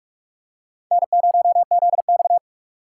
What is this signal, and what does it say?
Morse code, heard as a keyed tone, about 47 wpm, 700 Hz. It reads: N0ZX
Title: USA Call Signs - 2x Character Spacing - 45wpm